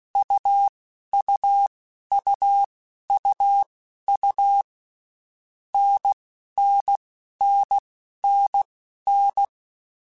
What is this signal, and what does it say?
Morse code, heard as a keyed tone, about 16 wpm, spaced as for 11 wpm, 800 Hz. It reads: UUUUU NNNNN
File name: Basic Letter Groups By7Groups / AWUNDG / AWUNDG@10